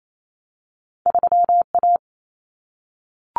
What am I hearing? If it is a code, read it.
Morse code, heard as a keyed tone, about 28 wpm, 700 Hz. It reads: 3 A
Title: Common Cut Numbers - 2-Digit 28wpm